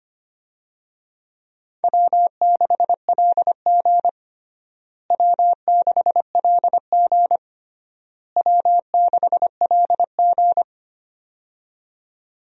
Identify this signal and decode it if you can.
Morse code, heard as a keyed tone, about 25 wpm, 700 Hz. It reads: W6LG W6LG W6LG
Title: USA Call Signs - 4 Characters - Repeated 3x - 25wpm